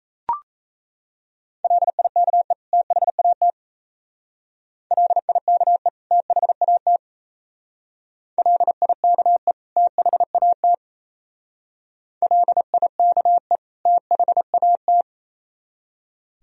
Morse code, heard as a keyed tone, about 36 wpm, 700 Hz. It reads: LIKE THAT LIKE THAT LIKE THAT LIKE THAT